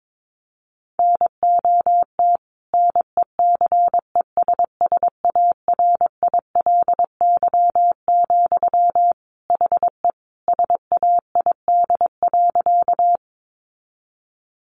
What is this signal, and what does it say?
Morse code, heard as a keyed tone, about 22 wpm, 700 Hz. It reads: NOT NECESSARILY, HE SAID.
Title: Sentences from Top 1400 Words 22wpm